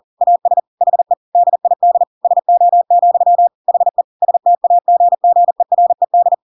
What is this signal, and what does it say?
Morse code, heard as a keyed tone, about 40 wpm, 700 Hz. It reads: AS HE DID SO, HE STAGGERED
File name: Pinocchio / Pinocchio ch7 40 40 ik1hge